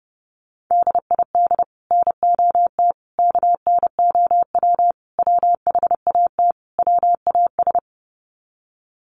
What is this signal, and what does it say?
Morse code, heard as a keyed tone, about 30 wpm, 700 Hz. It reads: DID NOT KNOW WHAT WAS
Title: Sets of 5 Words Encore2 30wpm